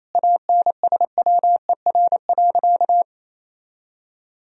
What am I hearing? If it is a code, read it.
Morse code, heard as a keyed tone, about 28 wpm, 700 Hz. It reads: ANSWER.
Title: Sentences from Top 1000 Words - Review Part 4 - 28wpm